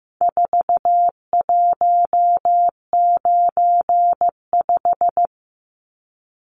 Morse code, heard as a keyed tone, about 15 wpm, 700 Hz. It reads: 4195
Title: Numbers - 4-Digit 15wpm